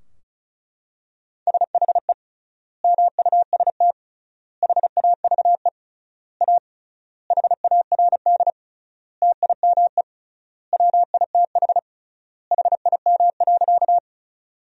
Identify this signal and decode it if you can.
Morse code, heard as a keyed tone, about 36 wpm, 700 Hz. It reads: SHE MUST HAVE A HARD TIME WITH HIM.
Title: 3x Word-Spacing - Sentences from Top 300 Words - 35wpm